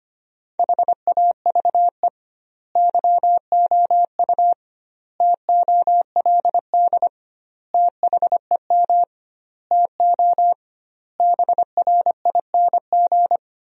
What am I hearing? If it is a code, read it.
Morse code, heard as a keyed tone, about 25 wpm, 700 Hz. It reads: HAVE YOU TOLD THEM TO BRING
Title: 2x Word-Spacing - Sentences from Top 500 Words - 25wpm